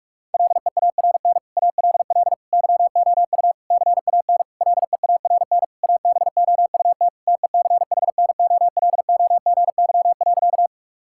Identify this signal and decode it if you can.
Morse code, heard as a keyed tone, about 41 wpm, 700 Hz. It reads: LEARN ALL YOU CAN LEARN ABOUT TECHNOLOGY.
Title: Sentences from Top 1000 Words - Review Part 5 - 45wpm